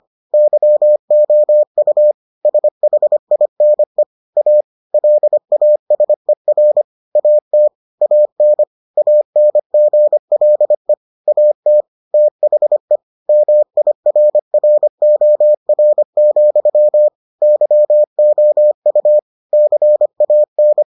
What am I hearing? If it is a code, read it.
Morse code, heard as a keyed tone, about 25 wpm, 600 Hz. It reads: YOU SHINE A LASER AT AN ANGLE AT THE MIRROR, YOU CAN